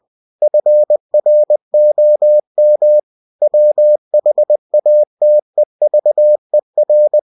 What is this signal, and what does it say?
Morse code, heard as a keyed tone, about 20 wpm, 600 Hz. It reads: FROM WHATEVER
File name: winnie-20wpm